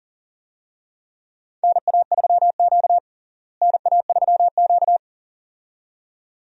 Morse code, heard as a keyed tone, about 40 wpm, 700 Hz. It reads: NA3Q NA3Q
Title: USA Call Signs - 4 Characters - Repeated 2x - 40wpm